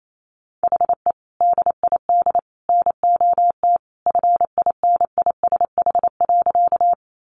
Morse code, heard as a keyed tone, about 28 wpm, 700 Hz. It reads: HE DID NOT FINISH.
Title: Sentences from Top 1200 Words 28wpm